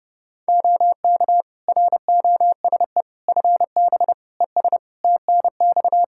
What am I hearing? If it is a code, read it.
Morse code, heard as a keyed tone, about 29 wpm, 700 Hz. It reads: OK ROSE FB ES TNX
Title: Practice QSO - Exchange 2 - Info - 30wpm